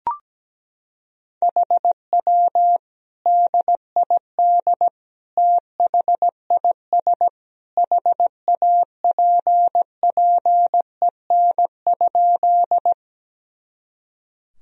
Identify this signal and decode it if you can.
Morse code, heard as a keyed tone, about 17 wpm, 700 Hz. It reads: HW DID THIS HAPPEN?
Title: Sentences from Top 400 Words and Abbreviations 17wpm